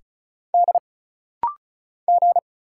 Morse code, heard as a keyed tone, about 36 wpm, 700 Hz. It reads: D G
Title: Single Letter-Number Alt - Rapid-Fire 35wpm